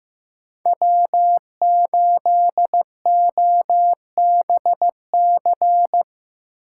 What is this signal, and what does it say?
Morse code, heard as a keyed tone, about 17 wpm, 700 Hz. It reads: W8OBC
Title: Speed Racing - USA Call Signs 15wpm